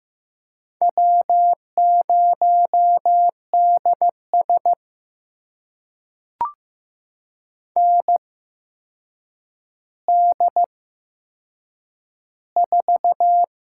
Morse code, heard as a keyed tone, about 15 wpm, spaced as for 7 wpm, 700 Hz. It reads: W0DS ND4
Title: USA Call Signs - 4 Characters - 8x Character Spacing - 15wpm